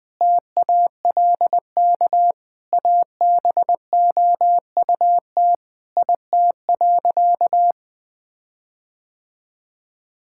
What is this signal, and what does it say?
Morse code, heard as a keyed tone, about 20 wpm, 700 Hz. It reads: TALK ABOUT IT.
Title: Sentences with 5-Word Phrases - Spoken Beginning - 20wpm